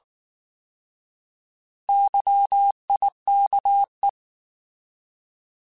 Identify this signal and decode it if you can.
Morse code, heard as a keyed tone, about 19 wpm, 800 Hz. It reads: YIKE
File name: Alphabetical / Alphabetical mp3s / Ys @18